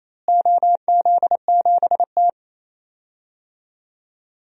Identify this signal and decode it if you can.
Morse code, heard as a keyed tone, about 27 wpm, 700 Hz. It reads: OZ7T
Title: DXCC-100 Call Signs - 4 Characters - Repeated 2x - 28wpm